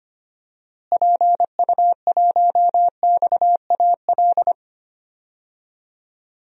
Morse code, heard as a keyed tone, about 25 wpm, 700 Hz. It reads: PU1XAL
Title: DXCC-100 Call Signs - 6 Characters - Repeated 3x - 25wpm